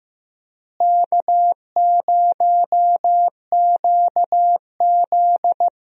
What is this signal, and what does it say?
Morse code, heard as a keyed tone, about 15 wpm, 700 Hz. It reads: K0QZ
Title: USA Call Signs - 4 Characters - Repeated 3x - 15wpm